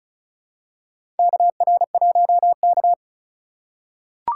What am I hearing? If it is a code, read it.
Morse code, heard as a keyed tone, about 36 wpm, 700 Hz. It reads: KR1K
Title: USA Call Signs - 4 Characters - 3x Character Spacing - 35wpm